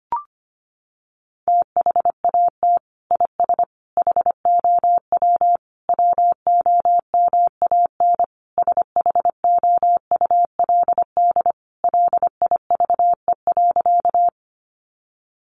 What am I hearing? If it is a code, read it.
Morse code, heard as a keyed tone, about 25 wpm, 700 Hz. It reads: THAT IS HOW WOMAN SHOULD LIVE.
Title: Sentences from Top 1100 Words 25wpm